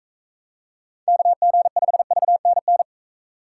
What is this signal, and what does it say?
Morse code, heard as a keyed tone, about 43 wpm, 700 Hz. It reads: KG5VNN